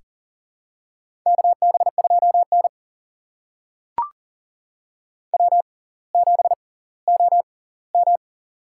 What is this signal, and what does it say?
Morse code, heard as a keyed tone, about 40 wpm, spaced as for 20 wpm, 700 Hz. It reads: KB2N W7OM
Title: USA Call Signs - 4 Characters - 6x Character Spacing - 40wpm